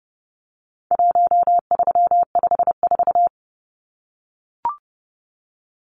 Morse code, heard as a keyed tone, about 30 wpm, 700 Hz. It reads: 1354
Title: Numbers - 4-Digit 30wpm